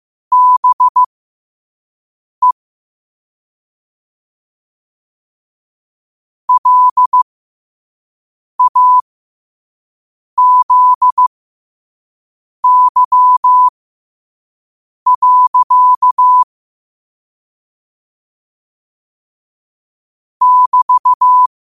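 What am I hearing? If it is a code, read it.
Morse code, heard as a keyed tone, about 15 wpm, spaced as for 5 wpm, 1000 Hz. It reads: BE LAZY. <BT>
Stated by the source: Morse practice at 5 wpm.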